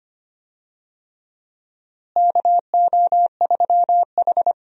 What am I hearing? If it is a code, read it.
Morse code, heard as a keyed tone, about 25 wpm, 700 Hz. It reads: KO3H
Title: USA Call Signs - 4 Characters - Repeated 3x - 25wpm